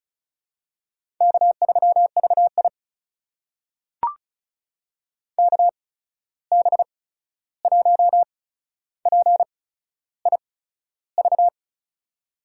Morse code, heard as a keyed tone, about 36 wpm, spaced as for 13 wpm, 700 Hz. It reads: K3VI KB1PIV
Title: USA Call Signs - 8x Character Spacing - 35wpm